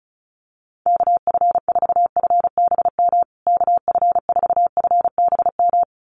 Morse code, heard as a keyed tone, about 34 wpm, 700 Hz. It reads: KF4FBM KF4FBM
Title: USA Call Signs Repeated 2x 35wpm